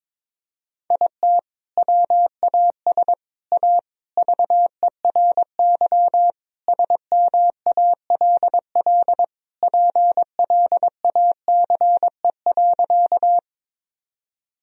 Morse code, heard as a keyed tone, about 22 wpm, 700 Hz. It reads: IT WAS A VERY SMALL PLACE.
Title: Sentences from Top 1000 Words - Review Part 1 - 22wpm